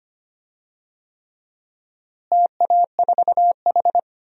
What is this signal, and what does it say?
Morse code, heard as a keyed tone, about 25 wpm, 700 Hz. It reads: TA4H
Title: DXCC-100 Call Signs - 4 Characters - Repeated 3x - 25wpm